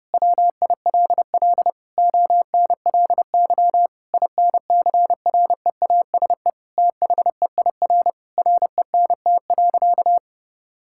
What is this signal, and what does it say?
Morse code, heard as a keyed tone, about 31 wpm, 700 Hz. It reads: WILL ONLY INCREASE THEIR RENT.
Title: Sentences from Top 1000 Words - Review Part 5 - 30wpm